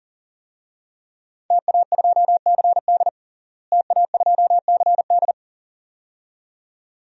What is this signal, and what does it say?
Morse code, heard as a keyed tone, about 40 wpm, 700 Hz. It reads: TA2CD TA2CD
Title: DXCC-100 Call Signs - 5 Characters - Repeated 2x - 40wpm